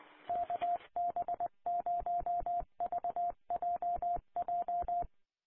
Morse code, heard as a keyed tone, about 20 wpm, 700 Hz. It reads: KB0VJJ
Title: Colorado Connection Statewide Linked Repeaters 447.525